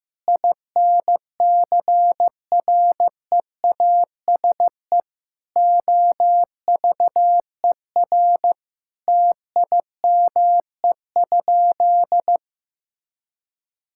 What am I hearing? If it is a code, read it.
Morse code, heard as a keyed tone, about 15 wpm, 700 Hz. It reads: INCREASE OVER TIME?
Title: Sentences from Top 1000 Words - Review Part 5 - 15wpm